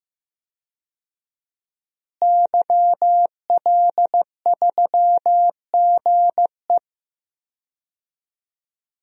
Morse code, heard as a keyed tone, about 15 wpm, 700 Hz. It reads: YL3GE
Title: DXCC-100 Call Signs - 5 Characters - Repeated 3x - 15wpm